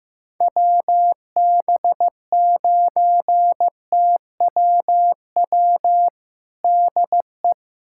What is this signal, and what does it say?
Morse code, heard as a keyed tone, about 15 wpm, 700 Hz. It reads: WB9TWW DE
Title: USA Callsign DE Callsign - 6 Characters - 15wpm